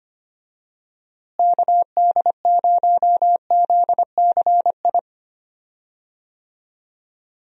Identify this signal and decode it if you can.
Morse code, heard as a keyed tone, about 25 wpm, 700 Hz. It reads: KD0ZCI
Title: USA Call Signs - 6 Characters - Repeated 3x - 25wpm